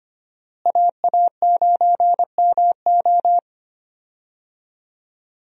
Morse code, heard as a keyed tone, about 25 wpm, 700 Hz. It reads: AA9MO
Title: USA Call Signs - 5 Characters - Repeated 2x - 25wpm